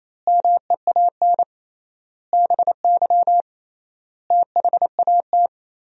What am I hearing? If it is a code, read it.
Morse code, heard as a keyed tone, about 28 wpm, 700 Hz. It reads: MEAN BY THAT
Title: 3x Word-Spacing - Sentences from Top 400 Words - 28wpm